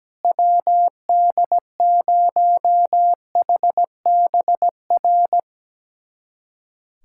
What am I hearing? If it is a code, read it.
Morse code, heard as a keyed tone, about 17 wpm, 700 Hz. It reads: WD0HBR
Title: USA Callsign DE Callsign - 6 Characters - 17wpm